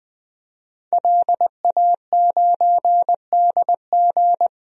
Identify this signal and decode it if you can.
Morse code, heard as a keyed tone, about 20 wpm, 700 Hz. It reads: LA9DG
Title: DXCC-100 Call Signs - 5 Characters - 20wpm